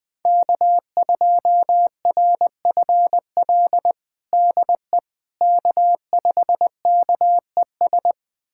Morse code, heard as a keyed tone, about 21 wpm, 700 Hz. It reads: K2RFL DE K5KES
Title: USA Callsign DE Callsign - 5 Characters - 20wpm